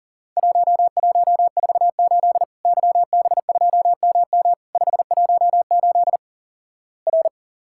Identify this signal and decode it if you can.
Morse code, heard as a keyed tone, about 43 wpm, 700 Hz. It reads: 1148 YB2MM 518 R
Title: Worked All Europe DX Contest - QTC Exchange - 40wpm